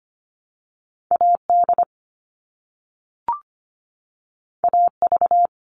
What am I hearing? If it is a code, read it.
Morse code, heard as a keyed tone, about 25 wpm, 700 Hz. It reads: AD AV
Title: All Cut Numbers - 2-Digit 25wpm